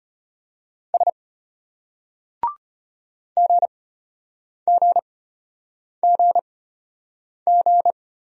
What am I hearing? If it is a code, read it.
Morse code, heard as a keyed tone, about 33 wpm, spaced as for 8 wpm, 700 Hz. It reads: S GGGG